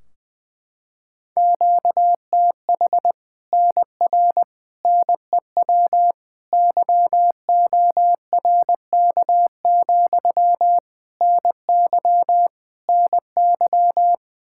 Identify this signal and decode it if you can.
Morse code, heard as a keyed tone, about 20 wpm, 700 Hz. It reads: QTH NR NEW YORK, NY NY
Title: Practice QSO - Exchange 1 - Beginning - 20wpm